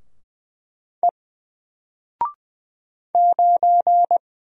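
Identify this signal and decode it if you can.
Morse code, heard as a keyed tone, about 20 wpm, 700 Hz. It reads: E 9
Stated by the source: Single letter and number Morse drill at 20 wpm, from N4LCD.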